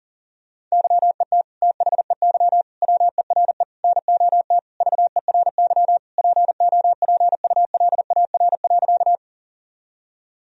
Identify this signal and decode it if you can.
Morse code, heard as a keyed tone, about 40 wpm, 700 Hz. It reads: YET THEY WERE NOT VERY POPULAR.
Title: Sentences from Top 1000 Words - Review Part 3 - 40wpm